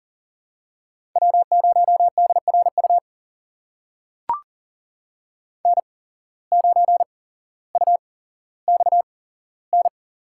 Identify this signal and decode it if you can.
Morse code, heard as a keyed tone, about 40 wpm, spaced as for 17 wpm, 700 Hz. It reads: W0DRU N9UXN
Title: USA Call Signs - 8x Character Spacing - 40wpm